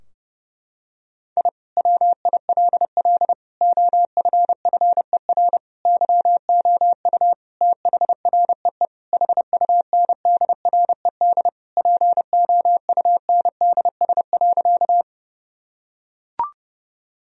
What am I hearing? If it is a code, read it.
Morse code, heard as a keyed tone, about 31 wpm, 700 Hz. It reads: I WILL OFFER YOU THREE HUNDRED POUNDS.
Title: Sentences from Top 1000 Words - Review Part 4 - 30wpm